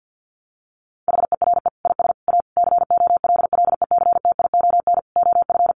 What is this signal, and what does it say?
Morse code, heard as a keyed tone, about 50 wpm, 700 Hz. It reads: HERE IS A COLLECTION OF
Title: Sentences from Top 1200 Words 50wpm